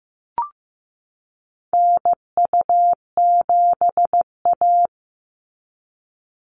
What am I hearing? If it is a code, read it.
Morse code, heard as a keyed tone, about 15 wpm, 700 Hz. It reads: NU7A
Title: NA SOTA Chaser Callsigns 15wpm